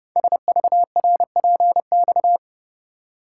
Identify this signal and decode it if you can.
Morse code, heard as a keyed tone, about 29 wpm, 700 Hz. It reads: SVRPX
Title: Code Groups - 5 Letters - 30wpm